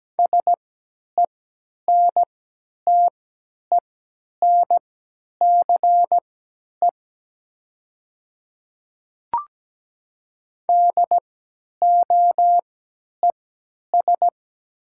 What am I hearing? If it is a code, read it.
Morse code, heard as a keyed tone, about 17 wpm, spaced as for 10 wpm, 700 Hz. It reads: SENTENCE DOES
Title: 3x Character Spacing - Top 200 Words - 17wpm